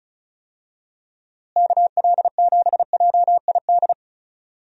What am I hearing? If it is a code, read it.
Morse code, heard as a keyed tone, about 36 wpm, 700 Hz. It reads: KL7JID
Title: DXCC-100 Call Signs - 6 Characters - Repeated 2x - 35wpm